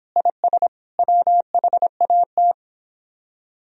Morse code, heard as a keyed tone, about 25 wpm, 700 Hz. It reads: IS WHAT